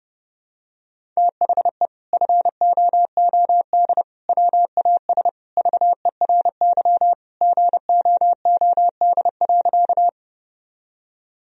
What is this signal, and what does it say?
Morse code, heard as a keyed tone, about 30 wpm, 700 Hz. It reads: THE FOOD WAS VERY GOOD.
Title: Sentences from Top 1000 Words - Review Part 1 - 30wpm